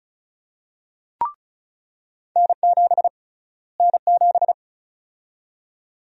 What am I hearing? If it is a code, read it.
Morse code, heard as a keyed tone, about 36 wpm, 700 Hz. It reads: N7 N7
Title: USA Call Sign Prefixes - Repeated 2x - 35wpm